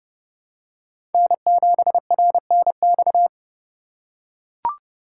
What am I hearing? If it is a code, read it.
Morse code, heard as a keyed tone, about 31 wpm, 700 Hz. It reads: N7RNX
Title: USA Call Signs - 5 Characters - 2x Character Spacing - 30wpm